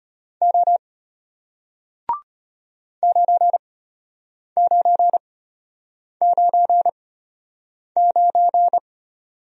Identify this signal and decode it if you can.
Morse code, heard as a keyed tone, about 34 wpm, spaced as for 8 wpm, 700 Hz. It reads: O 9999